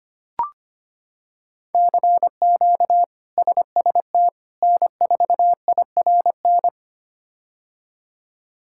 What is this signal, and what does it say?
Morse code, heard as a keyed tone, about 25 wpm, 700 Hz. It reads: CQ SST N4IRN
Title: K1USN SST QSO Practice - 25wpm